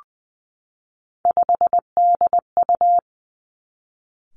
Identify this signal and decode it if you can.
Morse code, heard as a keyed tone, about 20 wpm, 700 Hz. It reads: 5DU
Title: Three Challenging Letter-Number Combinations 20wpm